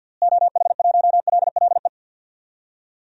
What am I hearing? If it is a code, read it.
Morse code, heard as a keyed tone, about 49 wpm, 700 Hz. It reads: OH1LLE
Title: DXCC-100 Call Signs - 6 Characters - Repeated 2x - 50wpm